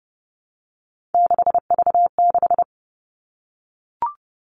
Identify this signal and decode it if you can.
Morse code, heard as a keyed tone, about 30 wpm, 700 Hz. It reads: 6V6
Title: Three Challenging Letter-Number Combinations 30wpm